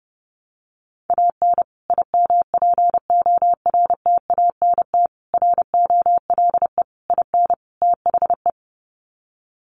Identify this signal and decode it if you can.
Morse code, heard as a keyed tone, about 30 wpm, 700 Hz. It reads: AN IMPORTANT ROLE IN THE
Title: Sets of 5 Words Encore2 30wpm